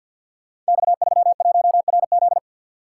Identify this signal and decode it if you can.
Morse code, heard as a keyed tone, about 53 wpm, 700 Hz. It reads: X31RZ
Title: Code Groups - 5 Letter:Numbers - 50wpm